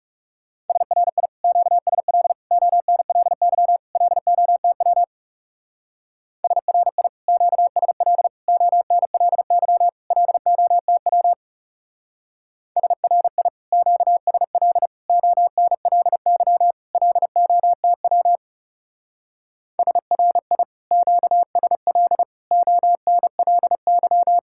Morse code, heard as a keyed tone, about 36 wpm, 700 Hz. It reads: SRI QSL ONLY LOTW SRI QSL ONLY LOTW SRI QSL ONLY LOTW SRI QSL ONLY
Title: Speed-Racing Common QSO Phrases Encore - 30wpm